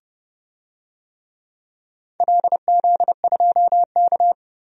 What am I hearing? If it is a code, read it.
Morse code, heard as a keyed tone, about 29 wpm, 700 Hz. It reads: LZ2K
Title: DXCC-100 Call Signs - 4 Characters - 30wpm